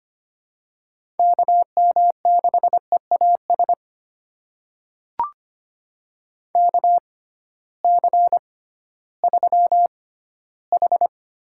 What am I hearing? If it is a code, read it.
Morse code, heard as a keyed tone, about 25 wpm, spaced as for 14 wpm, 700 Hz. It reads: KM6EAS KC3H
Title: USA Call Signs - 6 Characters - 6x Character Spacing - 25wpm